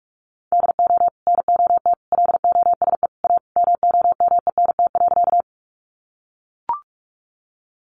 Morse code, heard as a keyed tone, about 44 wpm, 700 Hz. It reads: DO NOT LOSE A MOMENT.
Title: Sentences from Top 1100 Words 45wpm